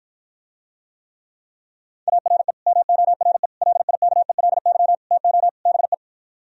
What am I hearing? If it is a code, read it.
Morse code, heard as a keyed tone, about 53 wpm, 700 Hz. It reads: ARE MORE LIKELY TO BE